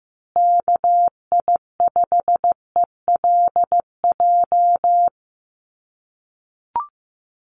Morse code, heard as a keyed tone, about 15 wpm, 700 Hz. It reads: KI5ELJ
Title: USA Call Signs Repeated 2x 15wpm